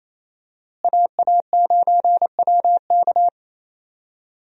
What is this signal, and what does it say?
Morse code, heard as a keyed tone, about 28 wpm, 700 Hz. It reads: AA9WK
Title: USA Call Signs - 5 Characters - Repeated 3x - 28wpm